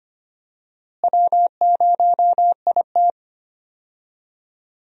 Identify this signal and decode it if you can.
Morse code, heard as a keyed tone, about 25 wpm, 700 Hz. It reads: W0IT
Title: USA Call Signs - 4 Characters - Repeated 2x - 25wpm